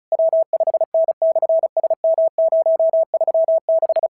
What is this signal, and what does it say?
Morse code, heard as a keyed tone, about 34 wpm, 650 Hz. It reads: W5N/SM036
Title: SOTA QSO Practice 35wpm